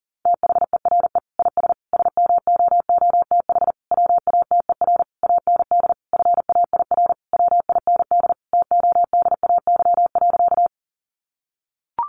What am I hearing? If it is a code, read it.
Morse code, heard as a keyed tone, about 40 wpm, 700 Hz. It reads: THERE IS SMOOTH WATER AND FAIR WIND TODAY.
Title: Sentences from Top 1400 Words 40wpm